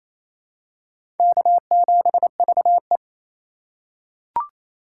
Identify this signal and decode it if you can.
Morse code, heard as a keyed tone, about 28 wpm, 700 Hz. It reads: K7VE
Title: USA Call Signs - 4 Characters - 6x Character Spacing - 28wpm